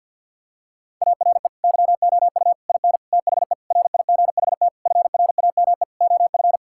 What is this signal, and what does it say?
Morse code, heard as a keyed tone, about 50 wpm, 700 Hz. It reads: ARE YOU IN THE RIGHT FRAME OF
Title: Sentences from Top 1000 Words - Review Part 6 - 50wpm